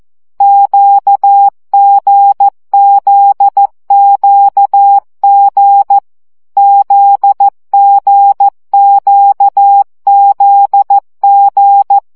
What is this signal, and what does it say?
Morse code, heard as a keyed tone, about 14 wpm, 800 Hz. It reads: QGZQG ZGQZG
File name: CW gyakorlas g z q 12 baud